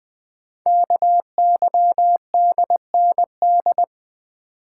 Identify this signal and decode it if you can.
Morse code, heard as a keyed tone, about 20 wpm, 700 Hz. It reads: KYDND
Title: Code Groups - 5 Letters - 20wpm